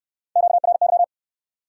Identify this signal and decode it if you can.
Morse code, heard as a keyed tone, about 68 wpm, 700 Hz. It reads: QRY